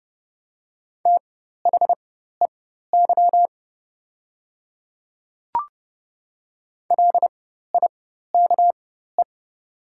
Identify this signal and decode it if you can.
Morse code, heard as a keyed tone, about 30 wpm, spaced as for 14 wpm, 700 Hz. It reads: THEY LIKE
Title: 4x Character Spacing - Top 200 Words - 30wpm